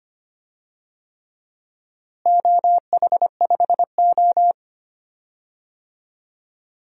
Morse code, heard as a keyed tone, about 25 wpm, 700 Hz. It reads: OH5O
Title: DXCC-100 Call Signs - 4 Characters - Repeated 3x - 25wpm